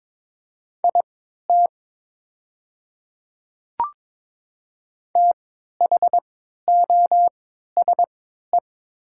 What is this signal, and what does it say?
Morse code, heard as a keyed tone, about 22 wpm, spaced as for 13 wpm, 700 Hz. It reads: IT THOSE
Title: 3x Character Spacing - Top 300 Words - 22wpm